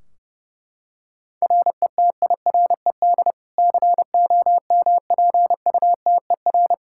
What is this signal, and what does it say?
Morse code, heard as a keyed tone, about 32 wpm, 700 Hz. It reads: RETIRED COMPUTER
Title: Practice QSO - Exchange 3 - More Info - 30wpm